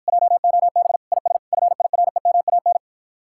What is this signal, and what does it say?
Morse code, heard as a keyed tone, about 54 wpm, 700 Hz. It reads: JOB IS FIREMAN